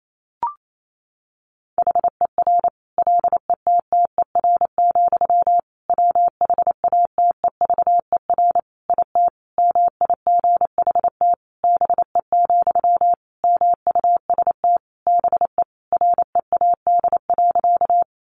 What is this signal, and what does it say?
Morse code, heard as a keyed tone, about 28 wpm, 700 Hz. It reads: HER LETTER, WHATEVER IT MIGHT BE, MUST BE READ.
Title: Sentences from Top 1100 Words 28wpm